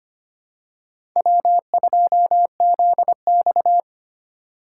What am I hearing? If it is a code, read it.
Morse code, heard as a keyed tone, about 25 wpm, 700 Hz. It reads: W2ZX
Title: USA Call Signs - 2x Character Spacing - 25wpm